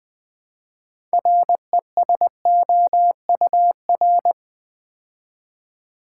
Morse code, heard as a keyed tone, about 20 wpm, 700 Hz. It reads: RESOUR
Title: Build A Word - 2 Letters At A Time - 8-Letter Words - 20wpm